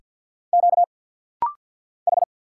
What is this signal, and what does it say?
Morse code, heard as a keyed tone, about 50 wpm, 700 Hz. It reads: Q H
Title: Single Letter-Number Alt - Rapid-Fire 50wpm